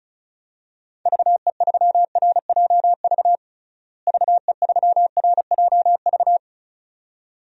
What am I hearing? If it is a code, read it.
Morse code, heard as a keyed tone, about 36 wpm, 700 Hz. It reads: VE3RJV VE3RJV
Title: DXCC-100 Call Signs - 6 Characters - Repeated 2x - 35wpm